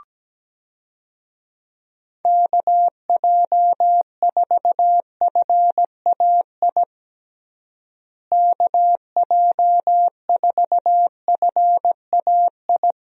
Morse code, heard as a keyed tone, about 17 wpm, 700 Hz. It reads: KJ4FAI KJ4FAI
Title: USA Call Signs - 6 Characters - Repeated 3x - 17wpm